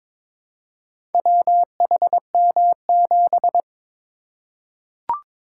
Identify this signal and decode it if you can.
Morse code, heard as a keyed tone, about 23 wpm, 690 Hz. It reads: WHM7
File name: practice / 22wpm / Four Letter-Number Combinations - 22wpm